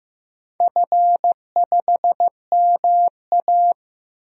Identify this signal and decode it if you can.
Morse code, heard as a keyed tone, about 15 wpm, 700 Hz. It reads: F5MA
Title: DXCC-100 Call Signs - 4 Characters - Repeated 3x - 15wpm